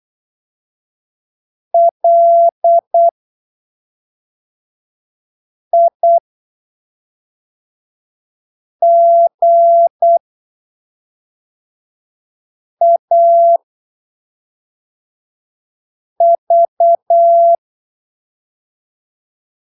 Morse code, heard as a keyed tone, about 8 wpm, spaced as for 3 wpm, 680 Hz. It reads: LIGAV
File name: arquivos / texto / texto 16 08ppm